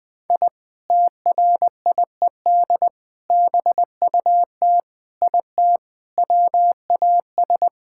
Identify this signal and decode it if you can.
Morse code, heard as a keyed tone, about 20 wpm, 700 Hz. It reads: I TRIED BUT IT WAS
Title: Sentences with 3-Word Phrases - 20wpm